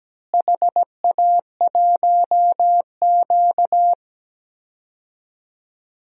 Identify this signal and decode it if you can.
Morse code, heard as a keyed tone, about 17 wpm, 700 Hz. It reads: HA1Q
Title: DXCC-100 Call Signs - 4 Characters - Repeated 3x - 17wpm